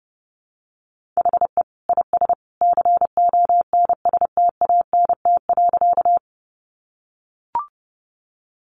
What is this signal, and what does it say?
Morse code, heard as a keyed tone, about 30 wpm, 700 Hz. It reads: HE IS CONSTANT.
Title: Sentences from Top 1500 Words 30wpm